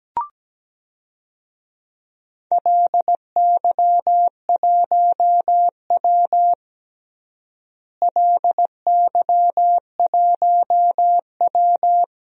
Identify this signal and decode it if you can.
Morse code, heard as a keyed tone, about 17 wpm, 700 Hz. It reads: LY1W LY1W
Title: DXCC-100 Call Signs - 4 Characters - Repeated 3x - 17wpm